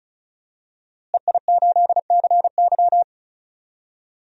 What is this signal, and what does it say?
Morse code, heard as a keyed tone, about 36 wpm, 700 Hz. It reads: EI8CY
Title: DXCC-100 Call Signs - 5 Characters - Repeated 3x - 35wpm